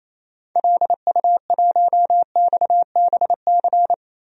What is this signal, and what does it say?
Morse code, heard as a keyed tone, about 27 wpm, 700 Hz. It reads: LU1XBC
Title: DXCC-100 Call Signs - 6 Characters - Repeated 3x - 28wpm